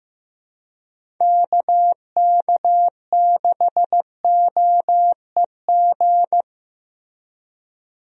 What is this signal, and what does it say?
Morse code, heard as a keyed tone, about 15 wpm, 700 Hz. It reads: KK6OEG
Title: USA Call Signs - 6 Characters - Repeated 3x - 15wpm